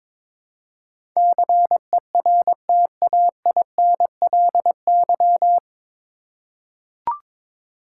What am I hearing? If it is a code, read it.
Morse code, heard as a keyed tone, about 22 wpm, 700 Hz. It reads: CERTAINLY
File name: 9 Letter Words - First 3 Letters Given - 22wpm